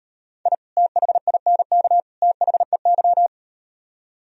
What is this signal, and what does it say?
Morse code, heard as a keyed tone, about 38 wpm, 700 Hz. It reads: I THINK THEY